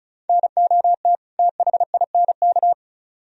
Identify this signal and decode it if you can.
Morse code, heard as a keyed tone, about 44 wpm, 700 Hz. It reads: NOT THINK
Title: Speed-Racing - Sets of 2 Words 35wpm